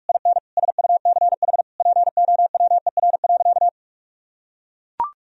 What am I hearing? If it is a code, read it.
Morse code, heard as a keyed tone, about 41 wpm, 700 Hz. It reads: IN SUCH POWER.
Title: Sentences from Top 1000 Words - Review Part 2 - 45wpm